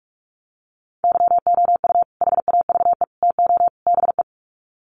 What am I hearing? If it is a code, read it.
Morse code, heard as a keyed tone, about 44 wpm, 700 Hz. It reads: YOU HAVE TO BE
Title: Sets of 4 Words 45wpm